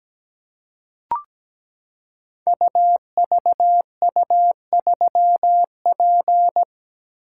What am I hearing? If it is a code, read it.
Morse code, heard as a keyed tone, about 17 wpm, 700 Hz. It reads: UVU3P
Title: Code Groups - 5 Letter:Numbers - 17wpm